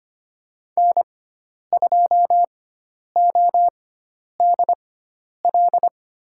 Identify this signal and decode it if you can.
Morse code, heard as a keyed tone, about 25 wpm, spaced as for 10 wpm, 700 Hz. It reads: N2ODL
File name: Lesson 16g - Letter c - Call Signs 25wpm